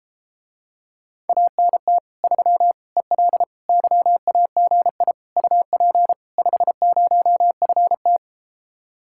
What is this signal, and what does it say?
Morse code, heard as a keyed tone, about 33 wpm, 700 Hz. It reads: ANT 3 EL YAGI UP 50FT